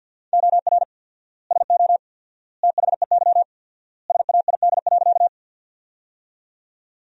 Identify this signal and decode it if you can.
Morse code, heard as a keyed tone, about 50 wpm, 700 Hz. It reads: OR SO THEY SAID.
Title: 4x Word-Spacing - Sentences from Top 100 Words - 50wpm